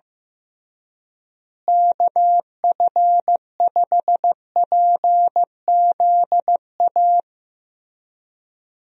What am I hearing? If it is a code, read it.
Morse code, heard as a keyed tone, about 15 wpm, 700 Hz. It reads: KF5PZA
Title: USA Call Signs - 6 Characters - Repeated 3x - 15wpm